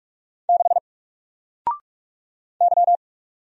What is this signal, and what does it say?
Morse code, heard as a keyed tone, about 44 wpm, 700 Hz. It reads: 6 Y
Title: Single Character (Letter:Number:Punctuation:Prosign) 45wpm